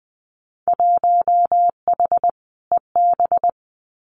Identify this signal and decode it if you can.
Morse code, heard as a keyed tone, about 20 wpm, 700 Hz. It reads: 1H EB
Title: Winter Field Day Exchanges Encore 20wpm